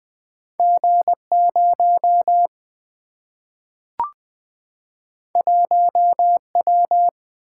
Morse code, heard as a keyed tone, about 20 wpm, 700 Hz. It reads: G0 1W
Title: Single-Letter Single-Number 20wpm